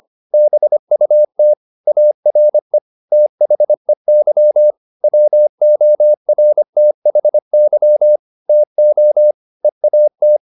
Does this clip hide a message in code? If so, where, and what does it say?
Yes: Morse code from 0.3 to 10.4 s, heard as a keyed tone, about 25 wpm, 600 Hz. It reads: BUT ARE THEY WORTHY TO EAT